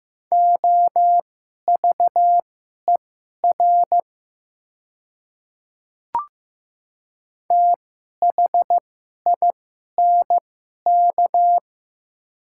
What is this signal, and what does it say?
Morse code, heard as a keyed tone, about 15 wpm, spaced as for 11 wpm, 700 Hz. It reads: OVER THINK
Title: 2x Character Spacing - Top 300 Words - 15wpm